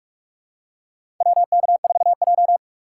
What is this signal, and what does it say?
Morse code, heard as a keyed tone, about 45 wpm, 700 Hz. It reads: WK4J
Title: USA Call Signs - 2x Character Spacing - 45wpm